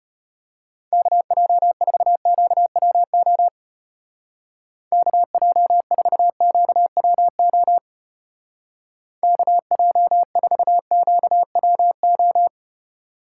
Morse code, heard as a keyed tone, about 32 wpm, 700 Hz. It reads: KJ4QWO KJ4QWO KJ4QWO